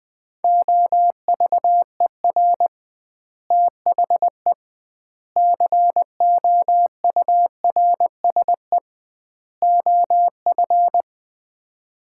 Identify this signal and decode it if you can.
Morse code, heard as a keyed tone, about 20 wpm, 700 Hz. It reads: OVER THE COURSE OF
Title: 2x Word-Spacing - Sets of 4 Words - 20wpm